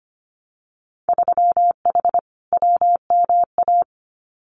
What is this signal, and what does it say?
Morse code, heard as a keyed tone, about 25 wpm, 690 Hz. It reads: 3H WMA
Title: Winter Field Day Exchanges Encore 25wpm